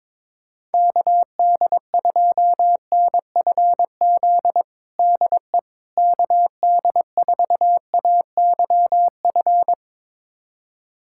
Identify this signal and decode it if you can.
Morse code, heard as a keyed tone, about 22 wpm, 700 Hz. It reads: KD2NFZ DE KD4AYF
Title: USA Callsign DE Callsign - 6 Characters - 22wpm